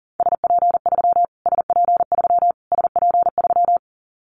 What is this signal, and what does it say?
Morse code, heard as a keyed tone, about 40 wpm, 700 Hz. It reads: SP3 SP3 SP3
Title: Callsign Prefixes for Top 25 DXCC - Repeated 3x - 40wpm